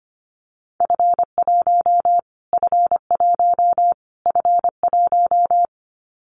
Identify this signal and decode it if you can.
Morse code, heard as a keyed tone, about 25 wpm, 700 Hz. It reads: F1 F1 F1
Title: Callsign Prefixes for Top 25 DXCC - Repeated 3x - 25wpm